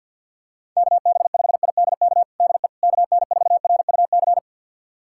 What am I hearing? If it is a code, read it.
Morse code, heard as a keyed tone, about 51 wpm, 700 Hz. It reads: KB5IDK DE KN4RUC
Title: USA Callsign DE Callsign - 6 Characters - 50wpm